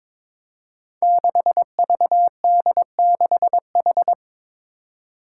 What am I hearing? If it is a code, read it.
Morse code, heard as a keyed tone, about 22 wpm, 700 Hz. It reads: 6VD6H
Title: Five Challenging Letter-Number Combinations 22wpm